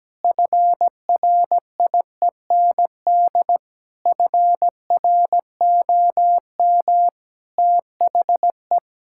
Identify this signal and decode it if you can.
Morse code, heard as a keyed tone, about 17 wpm, 700 Hz. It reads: FRIEND FROM THE
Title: Sentences from Top 1000 Words - Review Part 5 - 17wpm